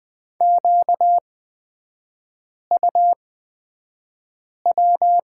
Morse code, heard as a keyed tone, about 20 wpm, spaced as for 5 wpm, 690 Hz. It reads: QUW
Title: 236 Lesson 38e - Letter q - Three Characters 20wpm